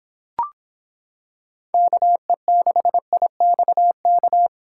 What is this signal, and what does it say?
Morse code, heard as a keyed tone, about 25 wpm, 700 Hz. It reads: KE6IXK